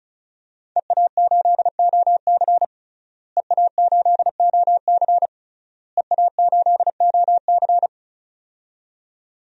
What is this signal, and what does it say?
Morse code, heard as a keyed tone, about 34 wpm, 700 Hz. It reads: EA8OC EA8OC EA8OC
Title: DXCC-100 Call Signs - 5 Characters - Repeated 3x - 35wpm